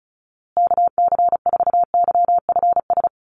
Morse code, heard as a keyed tone, about 34 wpm, 700 Hz. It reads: KC4YFS
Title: USA Call Signs Repeated 3x 35wpm